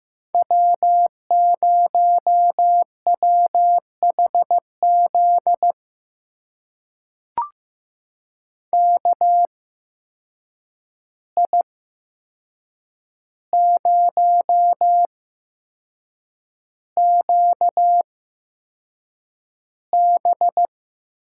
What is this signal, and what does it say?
Morse code, heard as a keyed tone, about 15 wpm, spaced as for 6 wpm, 700 Hz. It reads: W0WHZ KI0QB
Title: USA Call Signs - 5 Characters - 8x Character Spacing - 15wpm